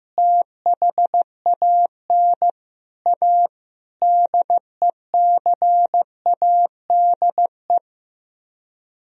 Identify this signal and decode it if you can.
Morse code, heard as a keyed tone, about 15 wpm, 700 Hz. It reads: THAN A DECADE
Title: Speed-Racing - Sets of 5 Words 15wpm